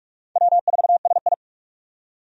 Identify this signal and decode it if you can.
Morse code, heard as a keyed tone, about 46 wpm, 700 Hz. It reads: W4SI
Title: USA Call Signs - 4 Characters - 45wpm